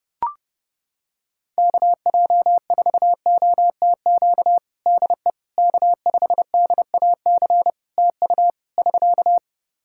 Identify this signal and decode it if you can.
Morse code, heard as a keyed tone, about 28 wpm, 700 Hz. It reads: KJ4OTQ DE K5DAC TU <SK>
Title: Practice QSO - Exchange 4 - Ending - 30wpm